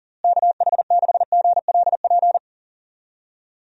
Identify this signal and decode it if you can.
Morse code, heard as a keyed tone, about 40 wpm, 700 Hz. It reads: KH6GLP
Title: DXCC-100 Call Signs - 6 Characters - Repeated 2x - 40wpm